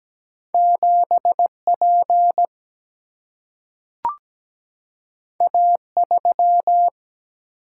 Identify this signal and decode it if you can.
Morse code, heard as a keyed tone, about 17 wpm, 700 Hz. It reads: 7P A3
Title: Single-Letter Single-Number 17wpm